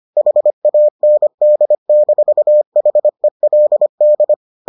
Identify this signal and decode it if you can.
Morse code, heard as a keyed tone, about 25 wpm, 600 Hz. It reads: HAND-HELD